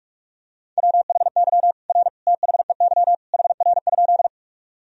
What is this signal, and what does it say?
Morse code, heard as a keyed tone, about 46 wpm, 700 Hz. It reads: WHY R THEY HR?
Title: Sentences from Top 200 Words and Abbreviations 45wpm